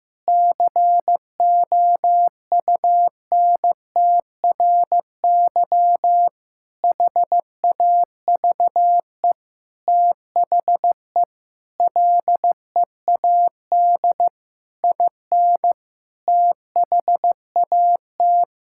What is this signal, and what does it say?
Morse code, heard as a keyed tone, about 15 wpm, 700 Hz. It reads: COUNTRY HAVE THE LEAD IN THAT
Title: Sentences from Top 1000 Words - Review Part 5 - 15wpm